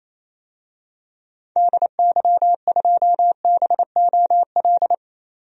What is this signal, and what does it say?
Morse code, heard as a keyed tone, about 28 wpm, 700 Hz. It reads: DY2BOL
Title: DXCC-100 Call Signs - 6 Characters - 28wpm